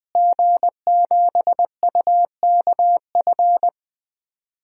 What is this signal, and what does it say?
Morse code, heard as a keyed tone, about 20 wpm, 700 Hz. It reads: G7UKF
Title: DXCC-100 Call Signs - 5 Characters - Repeated 2x - 20wpm